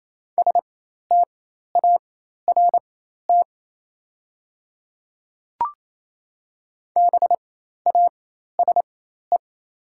Morse code, heard as a keyed tone, about 28 wpm, spaced as for 13 wpm, 700 Hz. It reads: START BASE
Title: 4x Character Spacing - Top 300 Words - 28wpm